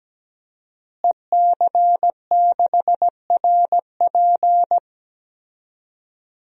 Morse code, heard as a keyed tone, about 17 wpm, 700 Hz. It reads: EC6RP
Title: DXCC-100 Call Signs - 5 Characters - Repeated 3x - 17wpm